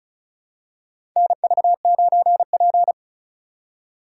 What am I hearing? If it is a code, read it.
Morse code, heard as a keyed tone, about 34 wpm, 700 Hz. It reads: NV9P
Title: USA Call Signs - 4 Characters - 2x Character Spacing - 35wpm